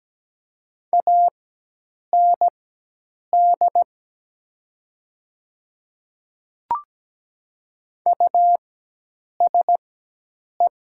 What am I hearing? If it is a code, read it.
Morse code, heard as a keyed tone, about 17 wpm, spaced as for 8 wpm, 700 Hz. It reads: AND USE
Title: 4x Character Spacing - Top 100 Words - 17wpm